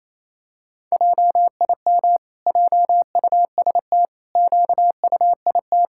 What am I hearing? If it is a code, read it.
Morse code, heard as a keyed tone, about 28 wpm, 700 Hz. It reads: JIM JUST QUIT
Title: Pangram Challenge v2 - 28wpm